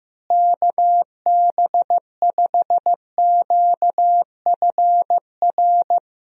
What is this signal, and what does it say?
Morse code, heard as a keyed tone, about 15 wpm, 700 Hz. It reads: KB5QFR
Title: USA Call Signs - 6 Characters - Repeated 3x - 15wpm